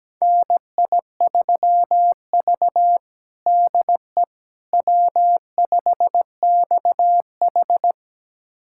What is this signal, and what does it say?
Morse code, heard as a keyed tone, about 17 wpm, 700 Hz. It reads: NI3V DE W5XH
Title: USA Callsign DE Callsign - 4 Characters - 17wpm